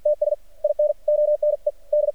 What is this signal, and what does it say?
Morse code, heard as a keyed tone, about 53 wpm, 600 Hz. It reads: TS IN ONE D
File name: archived / CRSnet